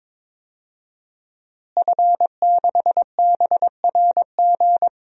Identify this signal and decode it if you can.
Morse code, heard as a keyed tone, about 22 wpm, 700 Hz. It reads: F6BRG
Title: DXCC-100 Call Signs - 5 Characters - Repeated 2x - 22wpm